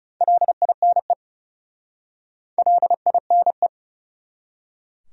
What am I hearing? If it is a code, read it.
Morse code, heard as a keyed tone, about 37 wpm, 700 Hz. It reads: LINE LINE
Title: Speed-Racing Top 300 Words 30wpm